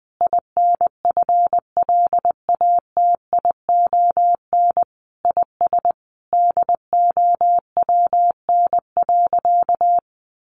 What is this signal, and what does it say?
Morse code, heard as a keyed tone, about 19 wpm, 700 Hz. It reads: INFLATION IS DOWN.
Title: Sentences from Top 1100 Words 20wpm